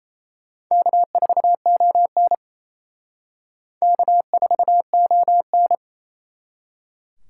Morse code, heard as a keyed tone, about 35 wpm, 700 Hz. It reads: K4ON K4ON